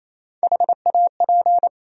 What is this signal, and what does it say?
Morse code, heard as a keyed tone, about 28 wpm, 700 Hz. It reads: HAP
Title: USA Call Sign Postfixes - 3 Letters 28wpm